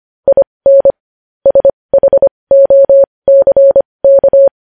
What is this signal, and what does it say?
Morse code, heard as a keyed tone, about 25 wpm, spaced as for 20 wpm, 550 Hz. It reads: IN SHOCK
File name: podcast / News headlines-1x-25x20